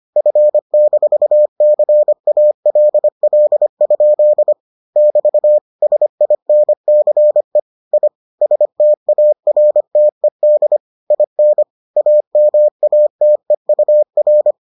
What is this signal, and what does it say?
Morse code, heard as a keyed tone, about 25 wpm, 600 Hz. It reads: F-CALL? <BT> SINCE I STARTED IN AMATEUR